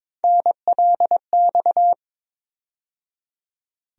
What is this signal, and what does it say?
Morse code, heard as a keyed tone, about 22 wpm, 700 Hz. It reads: NLX
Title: USA Call Sign Postfixes - 3 Letters 22wpm